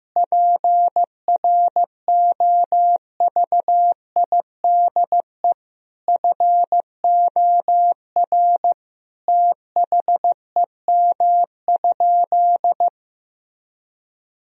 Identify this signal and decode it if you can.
Morse code, heard as a keyed tone, about 15 wpm, 700 Hz. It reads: PROVIDE FOR THEM?
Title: Sentences from Top 1000 Words - Review Part 3 - 15wpm